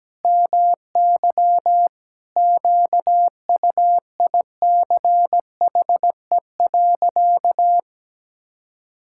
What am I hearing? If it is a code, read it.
Morse code, heard as a keyed tone, about 17 wpm, 700 Hz. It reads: MY QUICHE.
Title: Pangram Challenge v2 - 17wpm